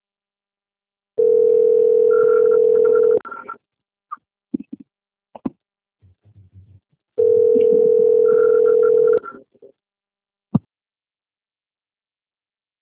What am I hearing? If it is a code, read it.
Ringback tone (North America), 440 + 480 Hz.